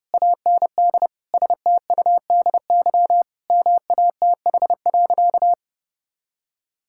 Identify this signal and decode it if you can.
Morse code, heard as a keyed tone, about 30 wpm, 700 Hz. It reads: AND STUDY MATH.
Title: Sentences from Top 1000 Words - Review Part 6 - 30wpm